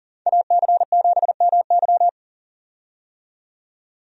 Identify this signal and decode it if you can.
Morse code, heard as a keyed tone, about 40 wpm, 700 Hz. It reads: AC7MY
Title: USA Call Signs - 5 Characters - 40wpm